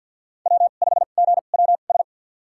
Morse code, heard as a keyed tone, about 52 wpm, spaced as for 36 wpm, 700 Hz. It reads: W5GWS
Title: USA Call Signs - 5 Characters - 2x Character Spacing - 50wpm